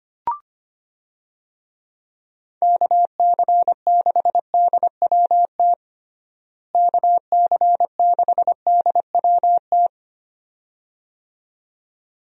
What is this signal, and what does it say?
Morse code, heard as a keyed tone, about 25 wpm, 700 Hz. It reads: KC6DWT KC6DWT
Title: USA Call Signs - 6 Characters - Repeated 2x - 25wpm